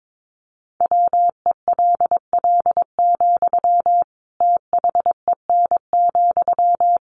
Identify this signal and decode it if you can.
Morse code, heard as a keyed tone, about 22 wpm, 700 Hz. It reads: WELL, THEN,
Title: Sentences from Top 700 Words 22wpm